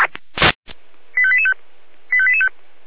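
Five-tone selective call (ZVEI), heard as quick runs of five tones, tones of about 70 ms. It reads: 85885 85885
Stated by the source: Received on SDR, ZVEI five-tone selective call 85830.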